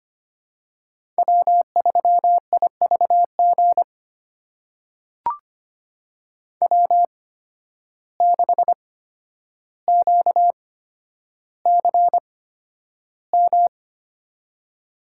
Morse code, heard as a keyed tone, about 25 wpm, spaced as for 11 wpm, 700 Hz. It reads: W3IVG W6QCM
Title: USA Call Signs - 8x Character Spacing - 25wpm